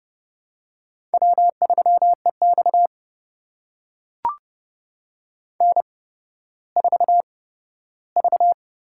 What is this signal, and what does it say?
Morse code, heard as a keyed tone, about 30 wpm, spaced as for 15 wpm, 700 Hz. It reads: W3EX N4V
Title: USA Call Signs - 4 Characters - 8x Character Spacing - 30wpm